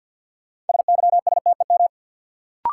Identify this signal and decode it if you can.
Morse code, heard as a keyed tone, about 48 wpm, 700 Hz. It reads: SYSTEM
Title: Sentences from Top 1000 Words - Review Part 3 - New Word Drill - 50wpm